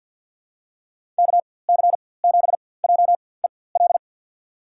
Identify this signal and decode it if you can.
Morse code, heard as a keyed tone, about 48 wpm, spaced as for 23 wpm, 700 Hz. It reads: KC7JEL
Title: USA Call Signs - 6 Characters - 4x Character Spacing - 50wpm